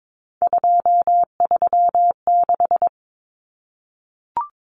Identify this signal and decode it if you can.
Morse code, heard as a keyed tone, about 22 wpm, 700 Hz. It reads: 236
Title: Numbers - 3-Digit 22wpm